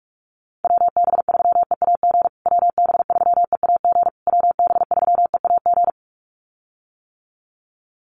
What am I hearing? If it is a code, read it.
Morse code, heard as a keyed tone, about 44 wpm, 700 Hz. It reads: WB3EAG WB3EAG WB3EAG
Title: USA Call Signs Repeated 3x 45wpm